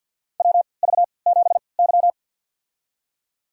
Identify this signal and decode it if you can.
Morse code, heard as a keyed tone, about 52 wpm, spaced as for 29 wpm, 700 Hz. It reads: WV7Y
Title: USA Call Signs - 4 Characters - 3x Character Spacing - 50wpm